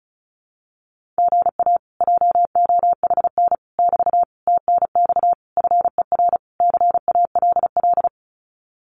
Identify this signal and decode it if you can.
Morse code, heard as a keyed tone, about 34 wpm, 700 Hz. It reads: GA JOHN <BT> TNX FER CALL
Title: Common QSO Phrases Encore 35wpm